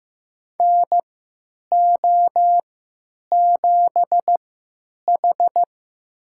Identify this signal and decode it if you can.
Morse code, heard as a keyed tone, about 16 wpm, spaced as for 9 wpm, 700 Hz. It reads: NO7H
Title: USA Call Signs - 4 Characters - 3x Character Spacing - 15wpm